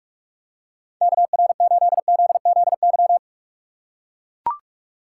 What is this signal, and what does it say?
Morse code, heard as a keyed tone, about 42 wpm, 700 Hz. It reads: KR8ZZY
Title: USA Call Signs - 6 Characters - 4x Character Spacing - 45wpm